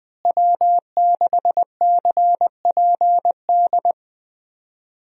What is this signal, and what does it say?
Morse code, heard as a keyed tone, about 21 wpm, 700 Hz. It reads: W6CPD
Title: USA Call Signs - 5 Characters - Repeated 3x - 20wpm